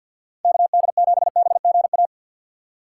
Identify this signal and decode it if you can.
Morse code, heard as a keyed tone, about 52 wpm, 690 Hz. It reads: KD7BGA
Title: USA Call Signs - 2x Character Spacing - 50wpm